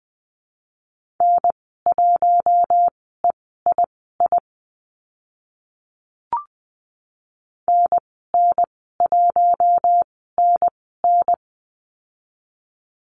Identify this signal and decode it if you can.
Morse code, heard as a keyed tone, about 20 wpm, spaced as for 14 wpm, 700 Hz. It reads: N1EII NN1NN
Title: 048 Lesson 9g - Number 4 - Call Signs 20wpm